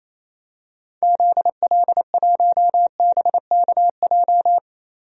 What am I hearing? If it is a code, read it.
Morse code, heard as a keyed tone, about 28 wpm, 700 Hz. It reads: ZL1BKJ
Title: DXCC-100 Call Signs - 6 Characters - Repeated 2x - 28wpm